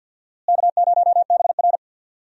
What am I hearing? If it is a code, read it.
Morse code, heard as a keyed tone, about 49 wpm, 700 Hz. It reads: K0BR
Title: USA Call Signs - 4 Characters - Repeated 2x - 50wpm